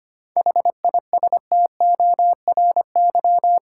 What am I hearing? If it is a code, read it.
Morse code, heard as a keyed tone, about 25 wpm, 700 Hz. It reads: HISTORY
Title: Sentences from Top 600 Words - New Word Drill - 25wpm